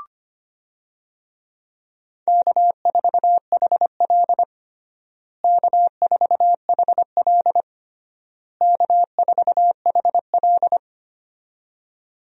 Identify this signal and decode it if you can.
Morse code, heard as a keyed tone, about 25 wpm, 700 Hz. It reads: K4HL K4HL K4HL
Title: USA Call Signs - 4 Characters - Repeated 3x - 25wpm